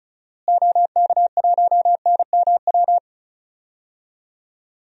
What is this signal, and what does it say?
Morse code, heard as a keyed tone, about 34 wpm, 700 Hz. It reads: OK1NMW
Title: DXCC-100 Call Signs - 6 Characters - Repeated 3x - 35wpm